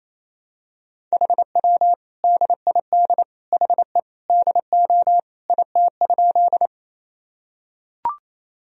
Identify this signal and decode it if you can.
Morse code, heard as a keyed tone, about 28 wpm, 700 Hz. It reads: HW DID HE DO IT?
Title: Sentences from Top 100 Words and Abbreviations 28wpm